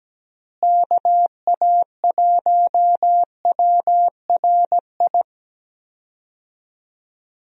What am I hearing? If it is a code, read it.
Morse code, heard as a keyed tone, about 17 wpm, 700 Hz. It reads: KA1WRI
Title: USA Call Signs - 6 Characters - Repeated 2x - 17wpm